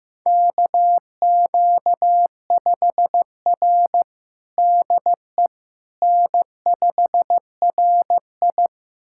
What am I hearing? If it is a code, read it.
Morse code, heard as a keyed tone, about 15 wpm, 700 Hz. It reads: KQ5R DE N5RI
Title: USA Callsign DE Callsign - 4 Characters - 15wpm